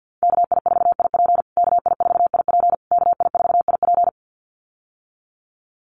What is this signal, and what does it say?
Morse code, heard as a keyed tone, about 50 wpm, 700 Hz. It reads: KI4IP KI4IP KI4IP
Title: USA Call Signs Repeated 3x 50wpm